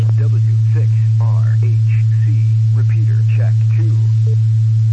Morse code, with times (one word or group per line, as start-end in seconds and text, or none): none